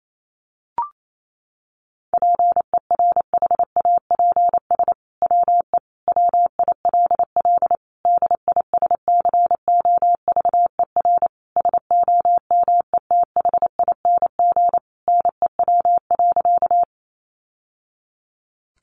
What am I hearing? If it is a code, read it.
Morse code, heard as a keyed tone, about 28 wpm, 700 Hz. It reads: PERHAPS WE WILL DISCOVER SOMETHING NEW.
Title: Sentences from Top 1100 Words 28wpm